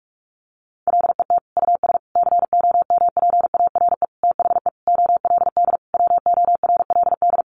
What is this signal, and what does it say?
Morse code, heard as a keyed tone, about 44 wpm, 700 Hz. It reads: LET US COMPARE THE OLD WORLD
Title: Sentences from Top 1300 Words 45wpm